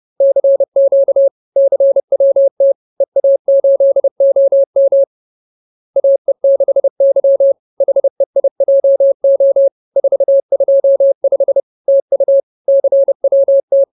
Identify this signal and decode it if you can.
Morse code, heard as a keyed tone, about 30 wpm, 550 Hz. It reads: CQ CWT EA8OM AE6Y HEIJO 425 TU CWT
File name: CWT-213-30